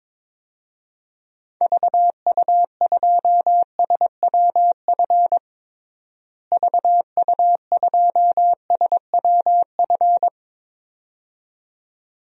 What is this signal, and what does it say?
Morse code, heard as a keyed tone, about 22 wpm, 700 Hz. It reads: VU2SWF VU2SWF
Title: DXCC-100 Call Signs - 6 Characters - Repeated 2x - 22wpm